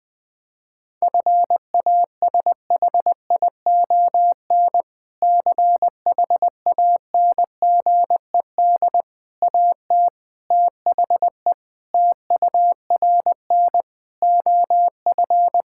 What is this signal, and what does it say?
Morse code, heard as a keyed tone, about 20 wpm, 700 Hz. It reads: FASHION CHANGED AT THE TURN OF
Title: Sentences with 5-Word Phrases - 20wpm